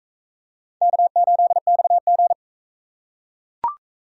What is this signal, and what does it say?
Morse code, heard as a keyed tone, about 38 wpm, 700 Hz. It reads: K8XG